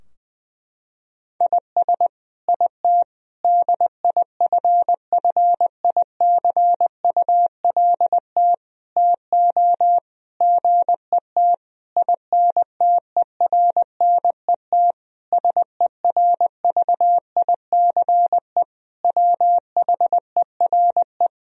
Morse code, heard as a keyed tone, about 21 wpm, 700 Hz. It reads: IS IT DIFFICULT TO GET INTERNET SERVICE WHERE
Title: Sentences from Top 1000 Words - Review Part 4 - 20wpm